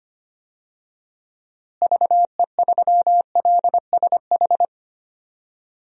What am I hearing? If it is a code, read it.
Morse code, heard as a keyed tone, about 25 wpm, 700 Hz. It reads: VE3LSH
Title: DXCC-100 Call Signs - 6 Characters - Repeated 3x - 25wpm